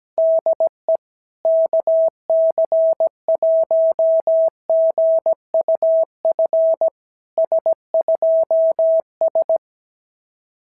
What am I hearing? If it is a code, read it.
Morse code, heard as a keyed tone, about 17 wpm, 650 Hz. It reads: DE KC1GUF S2S
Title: SOTA QSO Practice 17wpm